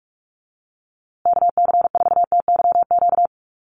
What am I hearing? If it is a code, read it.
Morse code, heard as a keyed tone, about 44 wpm, 690 Hz. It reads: KC4TYQ
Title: USA Call Signs Repeated 3x 45wpm